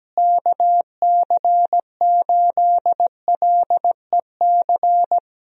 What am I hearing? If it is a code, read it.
Morse code, heard as a keyed tone, about 17 wpm, 700 Hz. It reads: KC8LEC
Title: USA Call Signs - 6 Characters - Repeated 3x - 17wpm